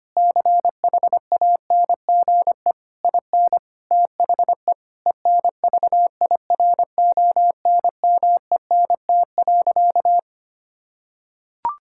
Morse code, heard as a keyed tone, about 25 wpm, 700 Hz. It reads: CHANGE IN THE ENVIRONMENT.
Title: Sentences from Top 1000 Words - Review Part 6 - 25wpm